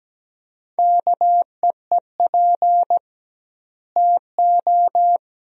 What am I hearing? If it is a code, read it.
Morse code, heard as a keyed tone, about 17 wpm, 700 Hz. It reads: KEEP TO
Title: 2x Word-Spacing - Sentences from Top 300 Words - 17wpm